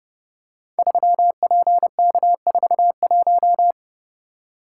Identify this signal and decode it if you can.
Morse code, heard as a keyed tone, about 32 wpm, 700 Hz. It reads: 3PK41
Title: Code Groups - 5 Letter:Numbers - 30wpm